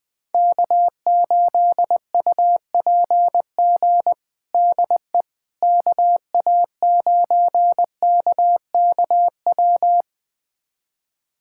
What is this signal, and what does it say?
Morse code, heard as a keyed tone, about 20 wpm, 700 Hz. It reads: K8UPG DE KA9KKW
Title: USA Callsign DE Callsign 20wpm